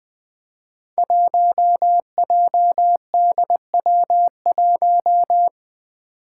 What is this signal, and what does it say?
Morse code, heard as a keyed tone, about 21 wpm, 700 Hz. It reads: 1JDW1
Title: Five Challenging Letter-Number Combinations 20wpm